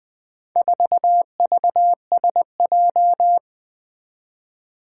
Morse code, heard as a keyed tone, about 20 wpm, 700 Hz. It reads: 4VSJ
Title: Four Challenging Letter-Number Combinations 20wpm